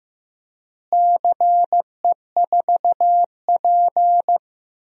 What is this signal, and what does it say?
Morse code, heard as a keyed tone, about 16 wpm, 700 Hz. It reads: CE4P
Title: DXCC-100 Call Signs - 4 Characters - Repeated 2x - 15wpm